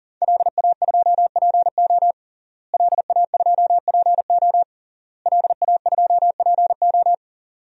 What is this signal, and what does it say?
Morse code, heard as a keyed tone, about 40 wpm, 700 Hz. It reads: LA2PO LA2PO LA2PO
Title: DXCC-100 Call Signs - 5 Characters - Repeated 3x - 40wpm